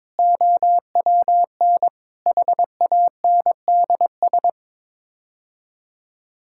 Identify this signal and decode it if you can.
Morse code, heard as a keyed tone, about 22 wpm, 700 Hz. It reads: OWN HANDS
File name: practice / 22wpm / Sets of 8 Words - 22wpm